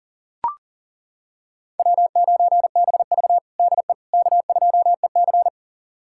Morse code, heard as a keyed tone, about 40 wpm, 700 Hz. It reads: W9BV DE K2EC
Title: USA Callsign DE Callsign - 4 Characters - 40wpm